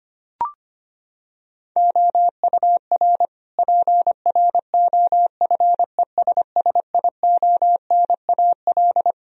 Morse code, heard as a keyed tone, about 25 wpm, 700 Hz. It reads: OUR PROFESSIONAL
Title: Sentences from Top 1000 Words - Review Part 4 - 25wpm